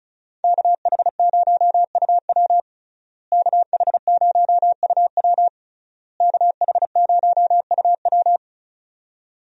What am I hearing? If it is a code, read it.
Morse code, heard as a keyed tone, about 35 wpm, 700 Hz. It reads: KH0UW KH0UW KH0UW
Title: USA Call Signs - 5 Characters - Repeated 3x - 35wpm